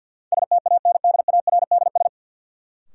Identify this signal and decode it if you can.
Morse code, heard as a keyed tone, about 53 wpm, 700 Hz. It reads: STANDARDS
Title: Sentences from Top 1000 Words - Review Part 6 - New Word Drill - 50wpm